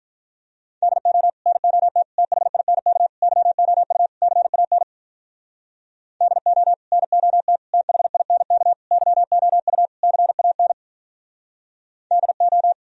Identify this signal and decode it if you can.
Morse code, heard as a keyed tone, about 43 wpm, 700 Hz. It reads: DO NOT THINK YOU CAN DO NOT THINK YOU CAN DO
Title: Speed-Racing - Sets of 5 Words 35wpm